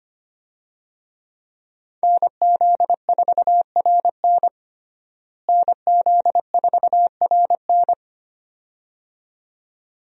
Morse code, heard as a keyed tone, about 25 wpm, 700 Hz. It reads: NZ4RN NZ4RN
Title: USA Call Signs - 5 Characters - Repeated 2x - 25wpm